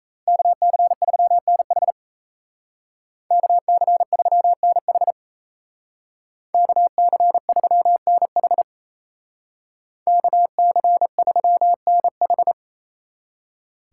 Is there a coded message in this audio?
Yes: Morse code from 0.3 to 12.5 s, heard as a keyed tone, about 36 wpm, 700 Hz. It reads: KC3NH KC3NH KC3NH KC3NH